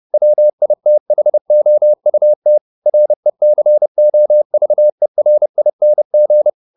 Morse code, heard as a keyed tone, about 30 wpm, 600 Hz. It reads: WITHOUT RECOVERING